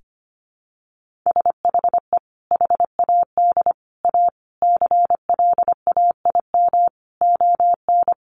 Morse code, heard as a keyed tone, about 25 wpm, 700 Hz. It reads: SHE HAD A CLAIM ON
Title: Sentences from Top 1500 Words 25wpm